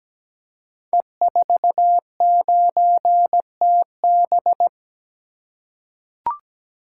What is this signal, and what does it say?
Morse code, heard as a keyed tone, about 17 wpm, 700 Hz. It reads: E49TB
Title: Code Groups - 5 Letter:Numbers - 17wpm